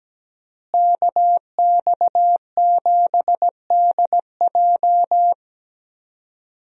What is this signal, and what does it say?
Morse code, heard as a keyed tone, about 17 wpm, 700 Hz. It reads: KX7DJ
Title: USA Call Signs - 5 Characters - Repeated 3x - 17wpm